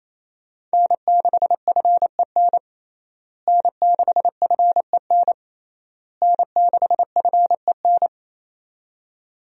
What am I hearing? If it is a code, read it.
Morse code, heard as a keyed tone, about 27 wpm, 700 Hz. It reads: N6FEN N6FEN N6FEN
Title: USA Call Signs - 5 Characters - Repeated 3x - 28wpm